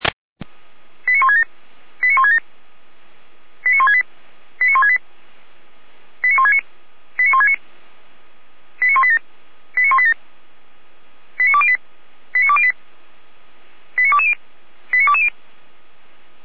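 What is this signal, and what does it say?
Five-tone selective call (ZVEI), heard as quick runs of five tones, tones of about 80 ms. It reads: 89167 89167 89168 89168 89169 89169 89187 89187 89208 89208 89220 89220
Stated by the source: Five-tone ZVEI selective call 89150 received on an SDR.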